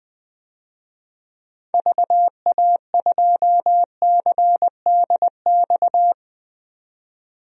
Morse code, heard as a keyed tone, about 20 wpm, 700 Hz. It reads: VA2CDX
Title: DXCC-100 Call Signs - 6 Characters - 20wpm